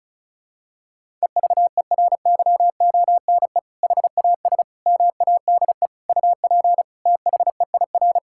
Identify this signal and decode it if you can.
Morse code, heard as a keyed tone, about 36 wpm, 700 Hz. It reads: EVERYONE HAS MADE UP THEIR
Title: Sentences from Top 1000 Words - Review Part 5 - 35wpm